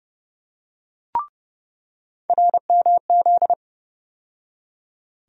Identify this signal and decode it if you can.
Morse code, heard as a keyed tone, about 30 wpm, 700 Hz. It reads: RMZ
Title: USA Call Sign Postfixes - 3 Letters 30wpm